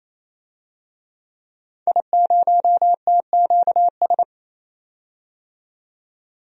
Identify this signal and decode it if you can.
Morse code, heard as a keyed tone, about 28 wpm, 700 Hz. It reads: I0TQS
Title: DXCC-100 Call Signs - 5 Characters - 28wpm